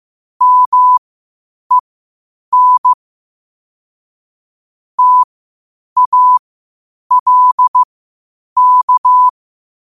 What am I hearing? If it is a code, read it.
Morse code, heard as a keyed tone, about 15 wpm, spaced as for 8 wpm, 1000 Hz. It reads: MEN TALK
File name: qotdfiles / 8wpm